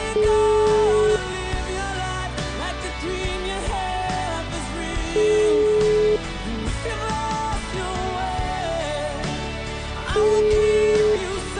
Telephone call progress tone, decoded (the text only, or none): ringback tone